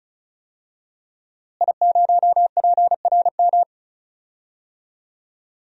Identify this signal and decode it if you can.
Morse code, heard as a keyed tone, about 34 wpm, 700 Hz. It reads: I0PRM
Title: DXCC-100 Call Signs - 5 Characters - Repeated 3x - 35wpm